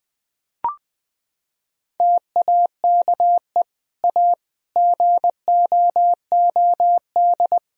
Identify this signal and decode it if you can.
Morse code, heard as a keyed tone, about 20 wpm, 700 Hz. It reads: TAKE A GOOD
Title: Sentences from Top 1000 Words - Review Part 3 - 20wpm